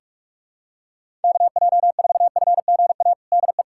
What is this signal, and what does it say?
Morse code, heard as a keyed tone, about 46 wpm, 700 Hz. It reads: KJ4FGA DE
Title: USA Callsign DE Callsign - 6 Characters - 45wpm